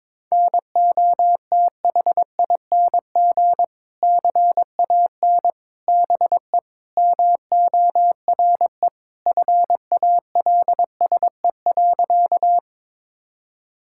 Morse code, heard as a keyed tone, about 22 wpm, 700 Hz. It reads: NOTHING CAN BE MORE FALSE.
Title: Sentences from Top 1600 Words 22wpm